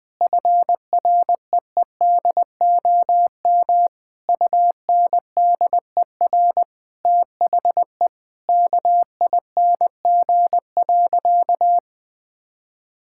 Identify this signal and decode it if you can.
Morse code, heard as a keyed tone, about 20 wpm, 700 Hz. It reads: FREEDOM UNDER THE KING.
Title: Sentences from Top 1000 Words - Review Part 6 - 20wpm